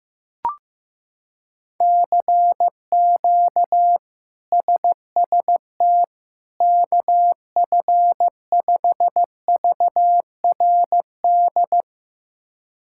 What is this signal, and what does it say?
Morse code, heard as a keyed tone, about 15 wpm, 700 Hz. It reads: CQ SST KF5VRD
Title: K1USN SST QSO Practice - 15wpm